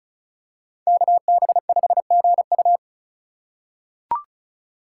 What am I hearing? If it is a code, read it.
Morse code, heard as a keyed tone, about 34 wpm, 700 Hz. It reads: KB5GU
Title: USA Call Signs - 5 Characters - 4x Character Spacing - 35wpm